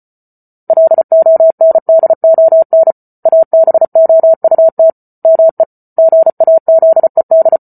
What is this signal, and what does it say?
Morse code, heard as a keyed tone, about 34 wpm, 650 Hz. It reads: LONDON ABOUT ME GAZED